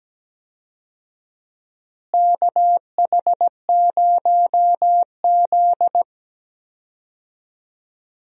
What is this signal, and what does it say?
Morse code, heard as a keyed tone, about 17 wpm, 700 Hz. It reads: KH0Z
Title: USA Call Signs - 4 Characters - Repeated 2x - 17wpm